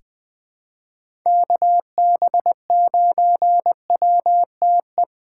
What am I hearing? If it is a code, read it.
Morse code, heard as a keyed tone, about 20 wpm, 700 Hz. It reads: KB9WTE
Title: USA Call Signs - 6 Characters - 2x Character Spacing - 20wpm